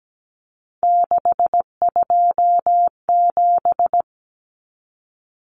Morse code, heard as a keyed tone, about 17 wpm, 700 Hz. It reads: 627
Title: Common Cut Numbers - 3-Digit 17wpm